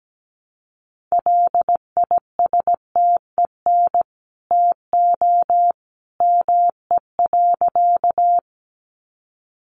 Morse code, heard as a keyed tone, about 17 wpm, 700 Hz. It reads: LISTEN TO ME.
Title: Sentences from Top 400 Words 17wpm